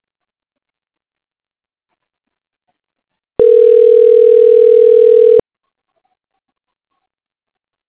Ringback tone (North America), 440 + 480 Hz.